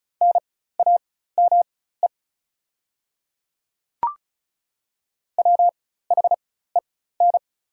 Morse code, heard as a keyed tone, about 36 wpm, spaced as for 16 wpm, 700 Hz. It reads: NAME WHEN
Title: 4x Character Spacing - Top 300 Words - 35wpm